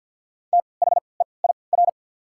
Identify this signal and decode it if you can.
Morse code, heard as a keyed tone, about 49 wpm, spaced as for 28 wpm, 690 Hz. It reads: THEIR
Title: 3x Character Spacing - Top 300 Words - 50wpm